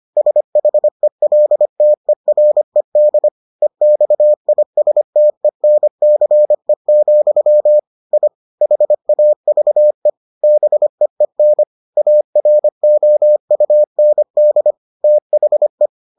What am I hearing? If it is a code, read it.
Morse code, heard as a keyed tone, about 25 wpm, 600 Hz. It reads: SHELTERED EXISTENCE, I HAVE BEEN AROUND THE